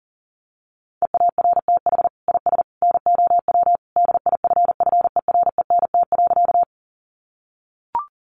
Morse code, heard as a keyed tone, about 40 wpm, 700 Hz. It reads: EARTH IS NOW DIFFERENT.
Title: Sentences from Top 600 Words 40wpm